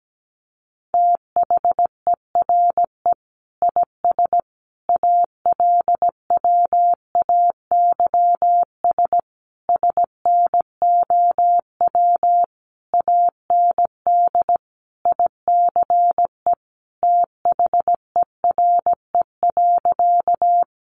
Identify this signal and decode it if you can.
Morse code, heard as a keyed tone, about 17 wpm, 700 Hz. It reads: THERE IS ALWAYS SNOW AND ICE THERE.
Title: Sentences from Top 1000 Words 17wpm